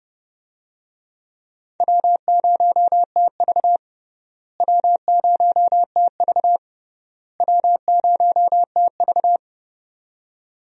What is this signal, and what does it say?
Morse code, heard as a keyed tone, about 30 wpm, 700 Hz. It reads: W0TV W0TV W0TV
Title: USA Call Signs - 4 Characters - Repeated 3x - 30wpm